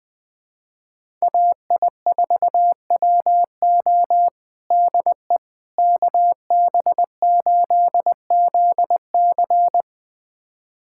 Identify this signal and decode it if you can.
Morse code, heard as a keyed tone, about 20 wpm, 700 Hz. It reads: AI4WO DE KB8ZC
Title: USA Callsign DE Callsign - 5 Characters - 20wpm